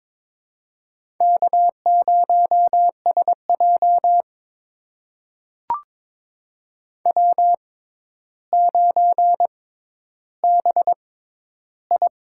Morse code, heard as a keyed tone, about 22 wpm, spaced as for 11 wpm, 700 Hz. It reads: K0SJ W9BI
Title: USA Call Signs - 4 Characters - 6x Character Spacing - 22wpm